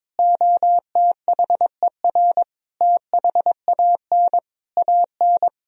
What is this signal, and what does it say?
Morse code, heard as a keyed tone, about 22 wpm, 700 Hz. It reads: OTHER THAN AN
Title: Sentences from Top 1000 Words - Review Part 2 - 22wpm